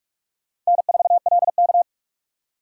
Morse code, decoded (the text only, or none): N4LK